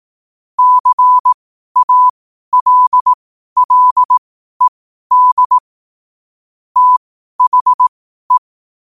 Morse code, heard as a keyed tone, about 18 wpm, spaced as for 12 wpm, 1000 Hz. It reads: CALLED THE
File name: qotdfiles / 12wpm